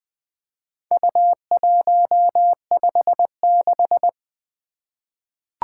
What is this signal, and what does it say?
Morse code, heard as a keyed tone, about 20 wpm, 700 Hz. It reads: U156
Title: Four Letter-Number Combinations - 20wpm